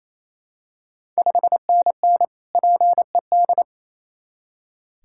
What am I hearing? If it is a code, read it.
Morse code, heard as a keyed tone, about 28 wpm, 700 Hz. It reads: 5NN PED
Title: Ontario QSO Party Exchange 28wpm